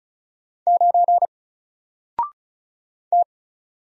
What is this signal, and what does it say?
Morse code, heard as a keyed tone, about 34 wpm, 700 Hz. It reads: 9 T
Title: Single Letter-Number Alt 35wpm